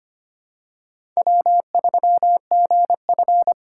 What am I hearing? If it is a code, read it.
Morse code, heard as a keyed tone, about 25 wpm, 700 Hz. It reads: W3GF
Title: USA Call Signs - 4 Characters - 6x Character Spacing - 25wpm